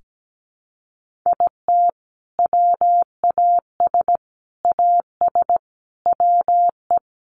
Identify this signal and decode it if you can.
Morse code, heard as a keyed tone, about 17 wpm, 700 Hz. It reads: IT WAS AS WE
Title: Sentences from Top 1500 Words 17wpm